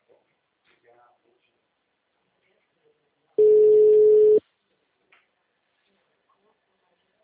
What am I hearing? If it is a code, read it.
Ringback tone (Europe, 425 Hz), with speech.